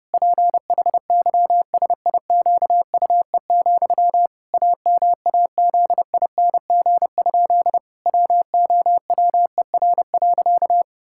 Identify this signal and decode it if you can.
Morse code, heard as a keyed tone, about 30 wpm, 700 Hz. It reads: PHYSIQUE, AMAZING? WOWER.
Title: Pangram Challenge v2 - 30wpm